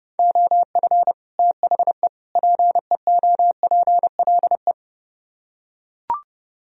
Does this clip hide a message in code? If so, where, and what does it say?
Yes: Morse code from 0.2 to 4.7 s, heard as a keyed tone, about 30 wpm, 700 Hz. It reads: OF THE PEOPLE